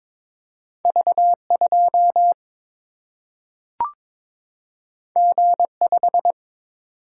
Morse code, heard as a keyed tone, about 22 wpm, 700 Hz. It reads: V 2 G 5
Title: Single-Letter Single-Number 22wpm